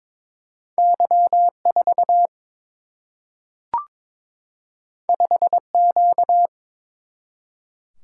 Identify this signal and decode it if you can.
Morse code, heard as a keyed tone, about 22 wpm, 700 Hz. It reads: Y 4 5 Q
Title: Single-Letter Single-Number 22wpm